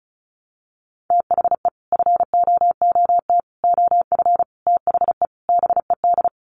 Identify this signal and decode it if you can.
Morse code, heard as a keyed tone, about 34 wpm, 700 Hz. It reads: THE FOOT OF THE BED
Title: Sets of 5 Words Encore2 35wpm